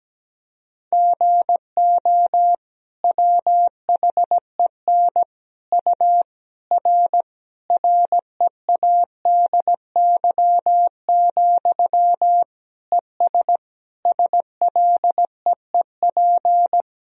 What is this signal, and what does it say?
Morse code, heard as a keyed tone, about 17 wpm, 700 Hz. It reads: GO WHEN U R READY, ES SLEEP
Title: Sentences from Top 500 Words and Abbreviations 17wpm